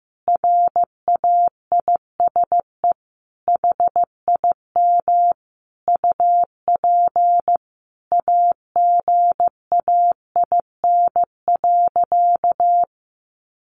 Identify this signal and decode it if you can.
Morse code, heard as a keyed tone, about 14 wpm, 700 Hz. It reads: RAISE HIM UP AGAIN.
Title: Sentences from Top 1400 Words 15wpm